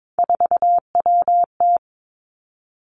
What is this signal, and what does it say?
Morse code, heard as a keyed tone, about 22 wpm, 700 Hz. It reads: 4WT
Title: Three Challenging Letter-Number Combinations 22wpm